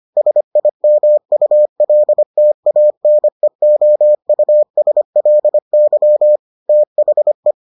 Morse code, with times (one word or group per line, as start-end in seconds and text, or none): 0.2-6.4 SIMULTANEOUSLY
6.7-7.5 THE